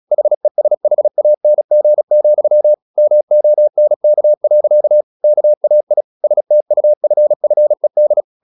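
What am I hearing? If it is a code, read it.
Morse code, heard as a keyed tone, about 36 wpm, 600 Hz. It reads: HESHANG, MONK. KAI STUFFED
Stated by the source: Morse practice at 35 wpm from CW med Gnister, sending Ken Follett's novel Never in English.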